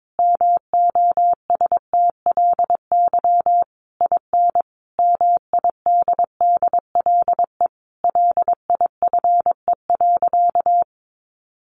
Morse code, heard as a keyed tone, about 22 wpm, 700 Hz. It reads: MOSTLY IN MIDDLE LIFE.
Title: Sentences from Top 1300 Words 22wpm